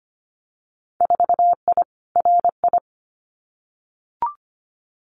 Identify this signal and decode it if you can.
Morse code, heard as a keyed tone, about 25 wpm, 700 Hz. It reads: 4I RI
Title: Winter Field Day Exchanges Encore 25wpm